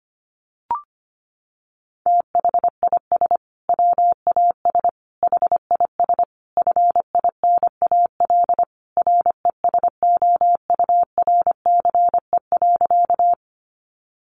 Morse code, heard as a keyed tone, about 25 wpm, 700 Hz. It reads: THIS WAS HIS FINAL RESOURCE.
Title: Sentences from Top 1300 Words 25wpm